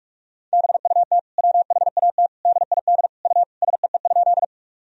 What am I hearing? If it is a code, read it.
Morse code, heard as a keyed tone, about 46 wpm, 700 Hz. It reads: BUT WHAT DID U SEE?
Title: Sentences from Top 100 Words and Abbreviations 45wpm